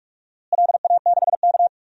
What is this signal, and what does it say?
Morse code, heard as a keyed tone, about 46 wpm, 700 Hz. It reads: LA6K
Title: DXCC-100 Call Signs - 4 Characters - Repeated 3x - 45wpm